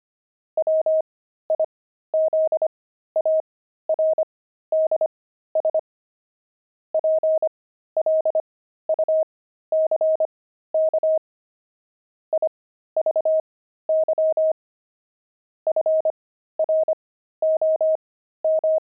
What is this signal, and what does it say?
Morse code, heard as a keyed tone, about 25 wpm, spaced as for 13 wpm, 630 Hz. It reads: WIZARDS PLUCK IVY FROM
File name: pangram-fs-13-cs-25-jinxed